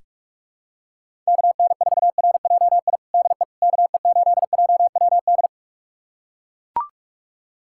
Morse code, heard as a keyed tone, about 41 wpm, 700 Hz. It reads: KN4RJI DE KE8JWD
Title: USA Callsign DE Callsign - 6 Characters - 45wpm